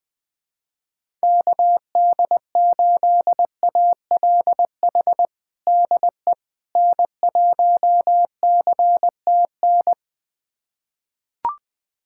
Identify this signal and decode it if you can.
Morse code, heard as a keyed tone, about 21 wpm, 700 Hz. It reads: KD8ALH DE N1CTN
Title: USA Callsign DE Callsign 20wpm